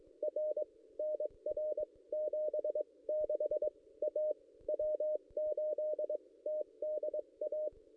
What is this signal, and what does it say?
Morse code, heard as a keyed tone, about 22 wpm, spaced as for 19 wpm, 610 Hz. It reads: R NR 76 A W8TDA
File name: contests / ss-cw-k5tr-op-n5rz -1600